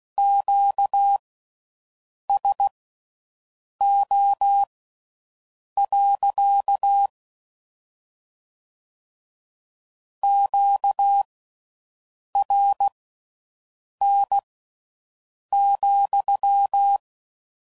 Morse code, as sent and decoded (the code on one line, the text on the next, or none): --.- ... --- .-.-.- / --.- .-. -. --..--
QSO. QRN,